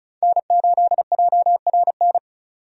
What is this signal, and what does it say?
Morse code, heard as a keyed tone, about 34 wpm, 700 Hz. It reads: N8JRN
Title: USA Call Signs - 5 Characters - 35wpm